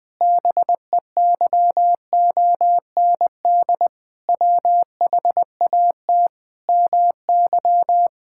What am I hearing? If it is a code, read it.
Morse code, heard as a keyed tone, about 21 wpm, 700 Hz. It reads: BEYOND WHAT MY
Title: Sentences from Top 1000 Words - Review Part 6 - 20wpm